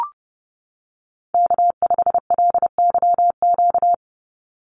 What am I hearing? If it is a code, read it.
Morse code, heard as a keyed tone, about 30 wpm, 700 Hz. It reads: K5LYQ
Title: USA Call Signs 30wpm